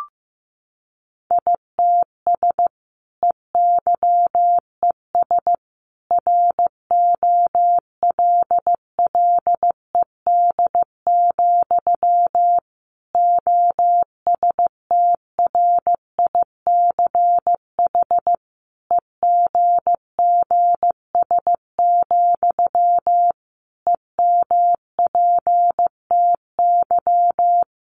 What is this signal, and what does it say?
Morse code, heard as a keyed tone, about 15 wpm, 700 Hz. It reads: ITS EYES ROLLED, OSTRICH EGGS, EMPTY
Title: A Sound Of Thunder 15wpm